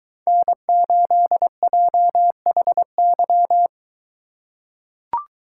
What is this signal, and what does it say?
Morse code, heard as a keyed tone, about 22 wpm, 700 Hz. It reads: N8JHY